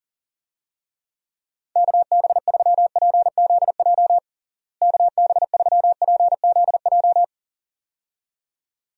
Morse code, heard as a keyed tone, about 40 wpm, 700 Hz. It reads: KB3PZJ KB3PZJ
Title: USA Call Signs - 6 Characters - Repeated 2x - 40wpm